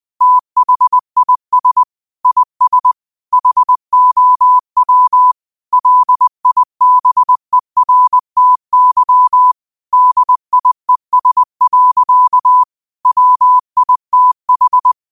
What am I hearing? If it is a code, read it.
Morse code, heard as a keyed tone, about 20 wpm, 1000 Hz. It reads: THIS IS HOW LIBERTY DIES. WITH
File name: qotdfiles / 20wpm